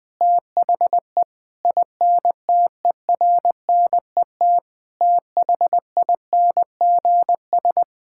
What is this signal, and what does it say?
Morse code, heard as a keyed tone, about 21 wpm, 700 Hz. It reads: THE INTERNET THINGS
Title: Sentences with 6-Word Phrases - Spoken Beginning - 20wpm